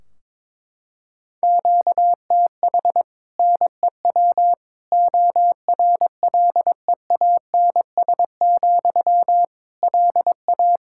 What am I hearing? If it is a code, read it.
Morse code, heard as a keyed tone, about 21 wpm, 700 Hz. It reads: QTH NEW ORLEANS, LA
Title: Practice QSO - Exchange 1 - Beginning - 22wpm